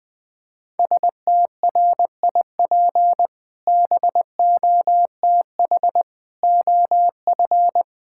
Morse code, heard as a keyed tone, about 21 wpm, 700 Hz. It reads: STRIP BOTH OF
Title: Sentences from Top 1600 Words 20wpm